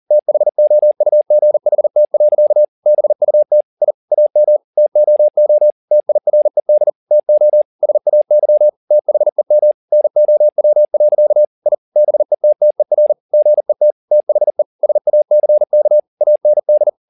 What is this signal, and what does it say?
Morse code, heard as a keyed tone, about 40 wpm, 600 Hz. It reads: THOUGHT. BUT I AM TOO TIRED TO SAY THEM NOW. I BETTER GET THE SACK AND
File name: the old man and the sea 2 40wpm